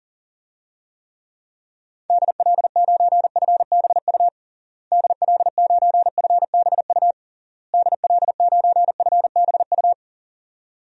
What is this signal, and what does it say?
Morse code, heard as a keyed tone, about 40 wpm, 700 Hz. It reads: DL9FBU DL9FBU DL9FBU
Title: DXCC-100 Call Signs - 6 Characters - Repeated 3x - 40wpm